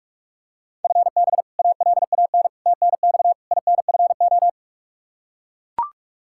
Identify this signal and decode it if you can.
Morse code, heard as a keyed tone, about 43 wpm, 700 Hz. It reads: FB ALAN TNX INFO
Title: Practice QSO - Exchange 3 - More Info - 45wpm